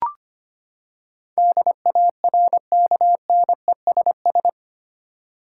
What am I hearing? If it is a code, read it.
Morse code, heard as a keyed tone, about 25 wpm, 700 Hz. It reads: DARKNESS
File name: Eight Letter Words - 25wpm